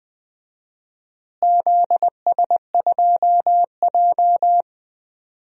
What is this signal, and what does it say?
Morse code, heard as a keyed tone, about 20 wpm, 700 Hz. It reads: ZS2J
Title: DXCC-100 Call Signs - 4 Characters - Repeated 3x - 20wpm